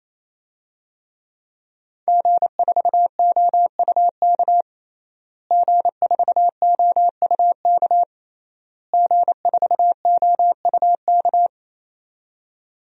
Morse code, heard as a keyed tone, about 28 wpm, 700 Hz. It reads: G4OUK G4OUK G4OUK
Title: DXCC-100 Call Signs - 5 Characters - Repeated 3x - 28wpm